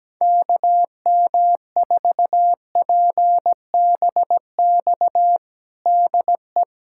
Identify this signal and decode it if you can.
Morse code, heard as a keyed tone, about 17 wpm, 700 Hz. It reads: KM4PBX DE
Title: USA Callsign DE Callsign - 6 Characters - 17wpm